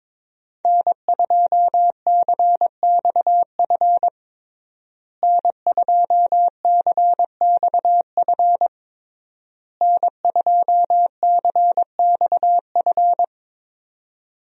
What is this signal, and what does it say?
Morse code, heard as a keyed tone, about 22 wpm, 700 Hz. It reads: N2CXF N2CXF N2CXF
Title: USA Call Signs - 5 Characters - Repeated 3x - 22wpm